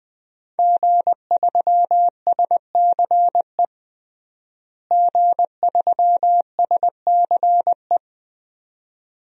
Morse code, heard as a keyed tone, about 20 wpm, 700 Hz. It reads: G3SCE G3SCE
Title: DXCC-100 Call Signs - 5 Characters - Repeated 2x - 20wpm